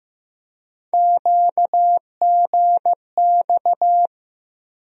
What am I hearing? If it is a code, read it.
Morse code, heard as a keyed tone, about 15 wpm, 700 Hz. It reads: QGX
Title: USA Call Sign Postfixes - 3 Letters 15wpm